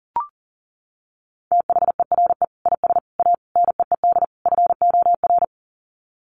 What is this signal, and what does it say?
Morse code, heard as a keyed tone, about 40 wpm, 700 Hz. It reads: THERE IS A NEED FOR
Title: Sets of 5 Words Encore2 40wpm